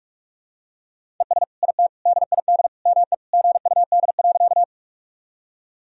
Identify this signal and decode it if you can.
Morse code, heard as a keyed tone, about 46 wpm, 700 Hz. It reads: ES IT DID ME GUD.
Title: Sentences from Top 200 Words and Abbreviations 45wpm